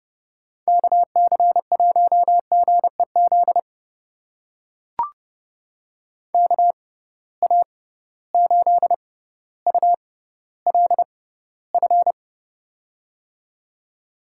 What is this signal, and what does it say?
Morse code, heard as a keyed tone, about 31 wpm, spaced as for 16 wpm, 700 Hz. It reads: KC1GEZ KA8ULF
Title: USA Call Signs - 6 Characters - 6x Character Spacing - 30wpm